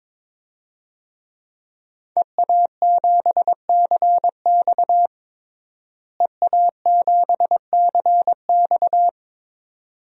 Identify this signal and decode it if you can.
Morse code, heard as a keyed tone, about 22 wpm, 700 Hz. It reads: EA7CX EA7CX
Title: DXCC-100 Call Signs - 5 Characters - Repeated 2x - 22wpm